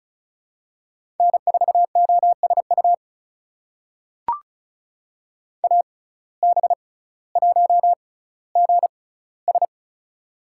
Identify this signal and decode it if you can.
Morse code, heard as a keyed tone, about 34 wpm, spaced as for 18 wpm, 700 Hz. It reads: N4OSU AB1GS
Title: USA Call Signs - 5 Characters - 6x Character Spacing - 35wpm